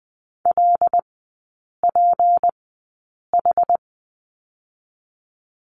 Morse code, heard as a keyed tone, about 20 wpm, spaced as for 8 wpm, 690 Hz. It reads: LPH
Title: 191 Lesson 31e - Letter v - Three Characters 20wpm